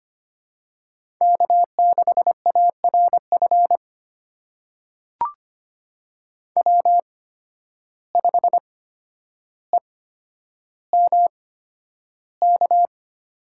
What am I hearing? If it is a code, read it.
Morse code, heard as a keyed tone, about 25 wpm, spaced as for 11 wpm, 700 Hz. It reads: K6ARF W5EMK
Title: USA Call Signs - 5 Characters - 8x Character Spacing - 25wpm